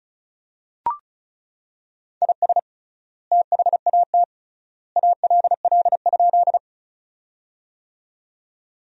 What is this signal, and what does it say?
Morse code, heard as a keyed tone, about 36 wpm, 700 Hz. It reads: IS THAT ALL?
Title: 3x Word-Spacing - Sentences from Top 100 Words - 35wpm